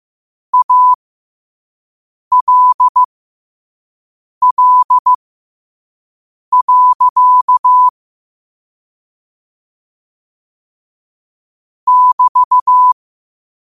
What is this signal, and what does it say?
Morse code, heard as a keyed tone, about 15 wpm, spaced as for 5 wpm, 1000 Hz. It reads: ALL. <BT>
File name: qotdfiles / 5wpm